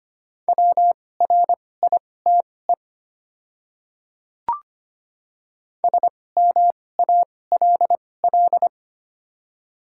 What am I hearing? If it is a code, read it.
Morse code, heard as a keyed tone, about 25 wpm, spaced as for 18 wpm, 700 Hz. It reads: WRITE SMALL
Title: 2x Character Spacing - Top 200 Words - 25wpm